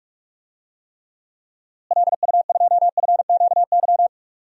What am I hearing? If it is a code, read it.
Morse code, heard as a keyed tone, about 46 wpm, 700 Hz. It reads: LU2FQY
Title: DXCC-100 Call Signs - 6 Characters - Repeated 2x - 45wpm